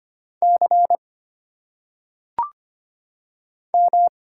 Morse code, heard as a keyed tone, about 25 wpm, 700 Hz. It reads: C M
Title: Single Letters - Phonetic Pronunciation 25wpm